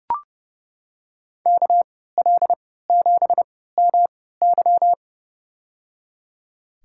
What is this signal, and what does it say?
Morse code, heard as a keyed tone, about 30 wpm, spaced as for 17 wpm, 700 Hz. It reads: KL7MY
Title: USA Call Signs - 5 Characters - 3x Character Spacing - 30wpm